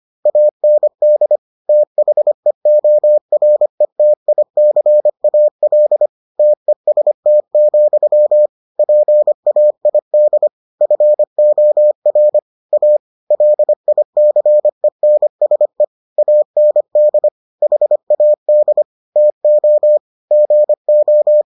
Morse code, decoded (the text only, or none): AND THEORETICAL TEST, PAID FOR A LICENSE AND HAD TO GO